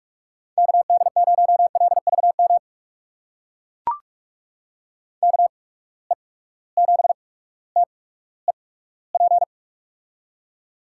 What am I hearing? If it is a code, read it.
Morse code, heard as a keyed tone, about 43 wpm, spaced as for 19 wpm, 700 Hz. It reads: KD0LVM KE7TEP
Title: USA Call Signs - 6 Characters - 8x Character Spacing - 45wpm